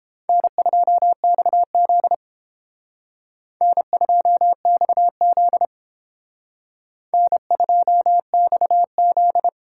Morse code, heard as a keyed tone, about 28 wpm, 700 Hz. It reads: N2XZ N2XZ N2XZ